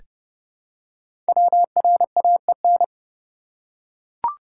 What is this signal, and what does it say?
Morse code, heard as a keyed tone, about 29 wpm, 700 Hz. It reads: WRAEN
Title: Code Groups - 5 Letters - 30wpm